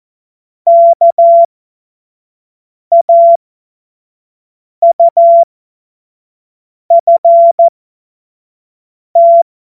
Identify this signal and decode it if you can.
Morse code, heard as a keyed tone, about 14 wpm, spaced as for 5 wpm, 680 Hz. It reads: KAUFT